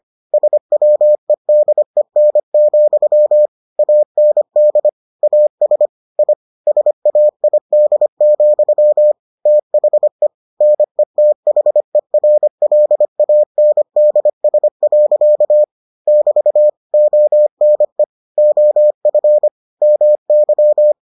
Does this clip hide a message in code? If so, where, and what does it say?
Yes: Morse code from 0.3 to 20.9 s, heard as a keyed tone, about 25 wpm, 600 Hz. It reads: SWEDEN, AND AS I SAID, THE NETHERLANDS. <BT> ONE OF MY